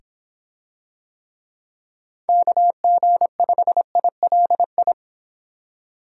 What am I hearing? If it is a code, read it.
Morse code, heard as a keyed tone, about 26 wpm, 700 Hz. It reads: KG5ILI